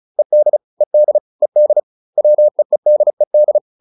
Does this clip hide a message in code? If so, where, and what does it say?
Yes: Morse code from 0.2 to 3.6 s, heard as a keyed tone, about 34 wpm, 600 Hz. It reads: ED ED ED WEEDED